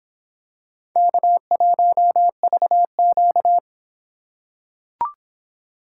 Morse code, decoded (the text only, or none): K1VQ